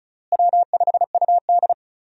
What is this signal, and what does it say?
Morse code, heard as a keyed tone, about 36 wpm, 700 Hz. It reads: W5UD
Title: USA Call Signs - 4 Characters - Repeated 3x - 35wpm